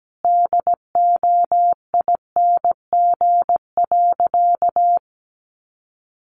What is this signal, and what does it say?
Morse code, heard as a keyed tone, about 17 wpm, 700 Hz. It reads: DOING.
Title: Sentences from Top 1400 Words 17wpm